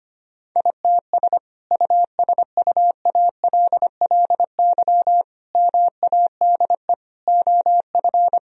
Morse code, heard as a keyed tone, about 25 wpm, 700 Hz. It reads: ITS USUALLY MADE OF
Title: Sentences from Top 1000 Words - Review Part 3 - 25wpm